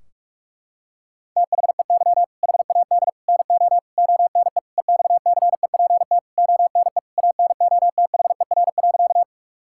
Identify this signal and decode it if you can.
Morse code, heard as a keyed tone, about 46 wpm, 700 Hz. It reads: THEY HAD NO ONE EXCEPT ONE ANOTHER.
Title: Sentences from Top 1600 Words 45wpm